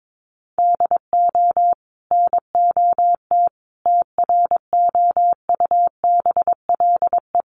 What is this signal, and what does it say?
Morse code, heard as a keyed tone, about 22 wpm, 700 Hz. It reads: DO NOT TROUBLE
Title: Sentences from Top 1200 Words 22wpm